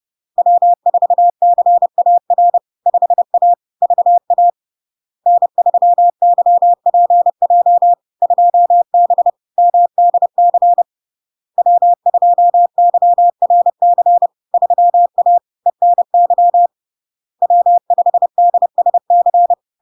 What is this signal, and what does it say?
Morse code, heard as a keyed tone, about 30 wpm, 700 Hz. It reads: W4CAR 5A VA N3YPJ 2B MDC W2YRC 3A ENY W5DSC
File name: LARK-FD-2022-Log-in-CW-30wpm-700-Hz